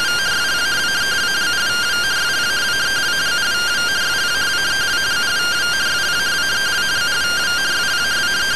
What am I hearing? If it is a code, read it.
Two-tone teleprinter signal, about 170 Hz shift.
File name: findings / CT AXIAL -142.gen-19 rtty rx